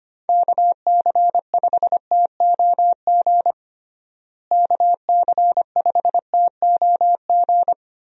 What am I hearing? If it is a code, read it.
Morse code, heard as a keyed tone, about 25 wpm, 700 Hz. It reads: KC5TOG KC5TOG
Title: USA Call Signs - 6 Characters - Repeated 2x - 25wpm